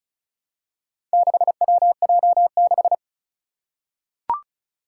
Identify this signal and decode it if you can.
Morse code, heard as a keyed tone, about 36 wpm, 690 Hz. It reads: 6WJ6
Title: Four Challenging Letter-Number Combinations 35wpm